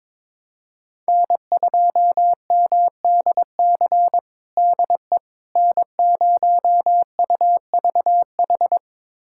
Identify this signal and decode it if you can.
Morse code, heard as a keyed tone, about 22 wpm, 700 Hz. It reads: N2MDC DE N0UVH
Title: USA Callsign DE Callsign - 5 Characters - 22wpm